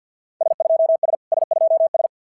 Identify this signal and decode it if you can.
Morse code, heard as a keyed tone, about 52 wpm, 690 Hz. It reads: S2S S2S
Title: SOTA QSO Practice 50wpm